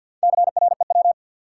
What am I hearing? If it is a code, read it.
Morse code, heard as a keyed tone, about 51 wpm, 700 Hz. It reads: CREW
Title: Sentences from Top 1500 Words - New Word Drill Alt - 50wpm